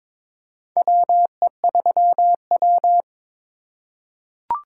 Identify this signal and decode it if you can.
Morse code, heard as a keyed tone, about 22 wpm, 700 Hz. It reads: WE3W
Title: USA Call Signs - 4 Characters - 4x Character Spacing - 22wpm